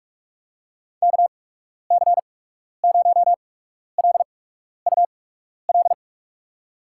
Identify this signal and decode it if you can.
Morse code, heard as a keyed tone, about 46 wpm, spaced as for 12 wpm, 700 Hz. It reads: KC0LUL
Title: USA Call Signs - 6 Characters - 8x Character Spacing - 45wpm